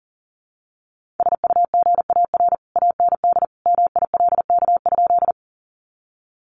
Morse code, heard as a keyed tone, about 40 wpm, 700 Hz. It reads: SUGAR AND MILK?
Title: Sentences from Top 1500 Words 40wpm